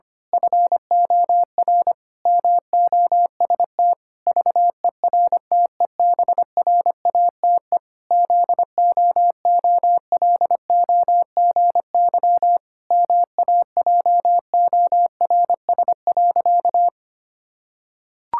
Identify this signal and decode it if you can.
Morse code, heard as a keyed tone, about 25 wpm, 700 Hz. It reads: FOR MOST VERTEBRATE ZOOLOGY MAJORS.
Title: Pangram Challenge v2 - 25wpm